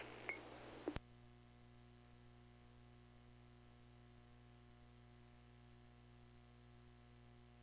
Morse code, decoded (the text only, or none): none